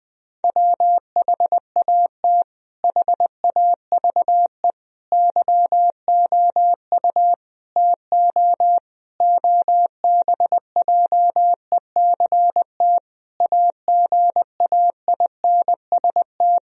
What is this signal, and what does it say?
Morse code, heard as a keyed tone, about 20 wpm, 700 Hz. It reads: WHAT HAVE YOU TO OBJECT AGAINST
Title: Sentences from Top 1000 Words - Review Part 3 - 20wpm